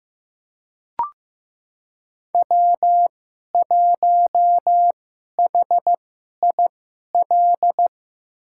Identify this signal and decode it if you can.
Morse code, heard as a keyed tone, about 16 wpm, spaced as for 11 wpm, 700 Hz. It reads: W1HIL
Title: USA Call Signs - 5 Characters - 2x Character Spacing - 15wpm